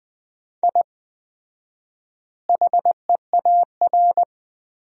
Morse code, heard as a keyed tone, about 20 wpm, 700 Hz. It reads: I HEAR
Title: 4x Word-Spacing - Sentences from Top 500 Words - 20wpm